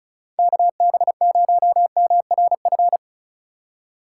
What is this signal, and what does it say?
Morse code, heard as a keyed tone, about 34 wpm, 690 Hz. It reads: KB0MRF
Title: USA Call Signs - 6 Characters - 4x Character Spacing - 35wpm